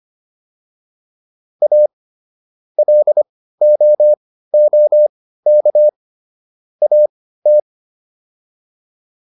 Morse code, heard as a keyed tone, about 25 wpm, spaced as for 15 wpm, 600 Hz. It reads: A LOOK AT